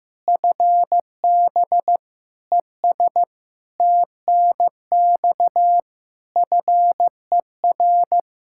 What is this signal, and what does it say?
Morse code, heard as a keyed tone, about 15 wpm, 700 Hz. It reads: FB ES TNX FER
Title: Practice QSO - Exchange 2 - Info - 15wpm